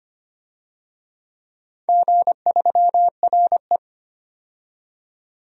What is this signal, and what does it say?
Morse code, heard as a keyed tone, about 25 wpm, 700 Hz. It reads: G3RE
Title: DXCC-100 Call Signs - 4 Characters - 25wpm